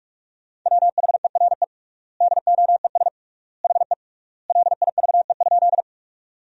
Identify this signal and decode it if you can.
Morse code, heard as a keyed tone, about 41 wpm, 700 Hz. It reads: WHERE DOES HE LIVE?
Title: 3x Word-Spacing - Sentences from Top 200 Words - 45wpm